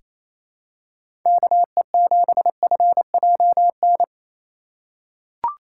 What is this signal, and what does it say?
Morse code, heard as a keyed tone, about 28 wpm, 700 Hz. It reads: KE7FJN
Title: USA Call Signs - 6 Characters - 3x Character Spacing - 28wpm